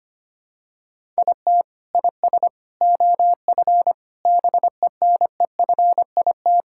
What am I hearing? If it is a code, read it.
Morse code, heard as a keyed tone, about 25 wpm, 700 Hz. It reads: IT IS OF BENEFIT
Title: Sentences from Top 1000 Words - Review Part 6 - 25wpm